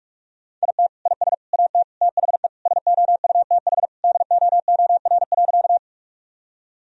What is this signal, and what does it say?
Morse code, heard as a keyed tone, about 42 wpm, 700 Hz. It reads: IT IS AT THE SOUTH DOOR.
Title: Sentences from Top 1000 Words - Review Part 2 - 45wpm